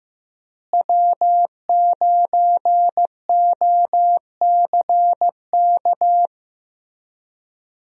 Morse code, heard as a keyed tone, about 15 wpm, 700 Hz. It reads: W9OCK
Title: USA Call Signs - 5 Characters - Repeated 2x - 15wpm